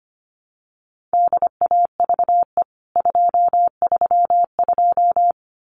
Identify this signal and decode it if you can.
Morse code, heard as a keyed tone, about 25 wpm, 700 Hz. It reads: DAVE 232
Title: CWT Exchange 25wpm